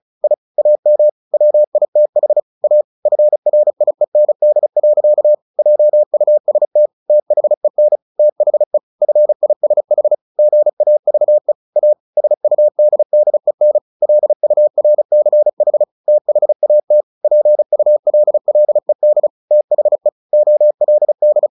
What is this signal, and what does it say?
Morse code, heard as a keyed tone, about 34 wpm, 600 Hz. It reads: I AM WITH A FRIEND. JUST THEN THE FISH GAVE A SUDDEN LURCH THAT PULLED THE OLD
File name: the old man and the sea / the old man and the sea 1 35wpm